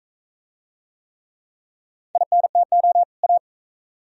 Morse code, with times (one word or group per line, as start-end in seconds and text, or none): 2.1-3.0 INTO
3.2-3.4 A